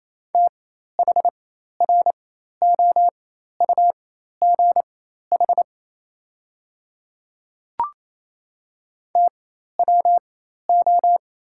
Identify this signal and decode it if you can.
Morse code, heard as a keyed tone, about 28 wpm, spaced as for 13 wpm, 690 Hz. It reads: THROUGH TWO
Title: 4x Character Spacing - Top 200 Words - 28wpm